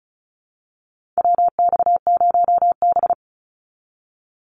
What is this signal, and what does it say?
Morse code, heard as a keyed tone, about 34 wpm, 700 Hz. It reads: WX0B
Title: Call Signs Repeated 3x - 35wpm